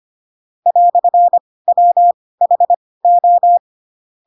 Morse code, heard as a keyed tone, about 25 wpm, spaced as for 18 wpm, 700 Hz. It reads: "WHO
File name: Pinocchio / Pinocchio ch2 12 25 ik1hge